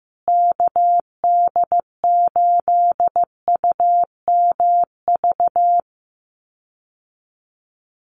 Morse code, heard as a keyed tone, about 15 wpm, 700 Hz. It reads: KD8UMV
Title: USA Call Signs Repeated 2x 15wpm